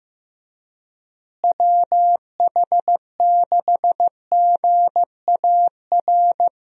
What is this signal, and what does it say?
Morse code, heard as a keyed tone, about 15 wpm, 700 Hz. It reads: WH6GAR
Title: USA Call Signs - 6 Characters - Repeated 3x - 15wpm